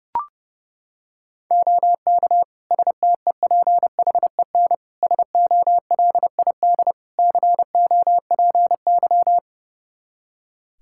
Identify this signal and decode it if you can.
Morse code, heard as a keyed tone, about 32 wpm, 700 Hz. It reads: OK STEPHEN SOLID COPY
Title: K1USN SST QSO Practice 30wpm